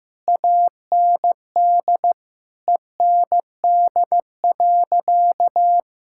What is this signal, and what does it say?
Morse code, heard as a keyed tone, about 15 wpm, 700 Hz. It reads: AND END.
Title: Sentences from Top 1000 Words - Review Part 6 - 15wpm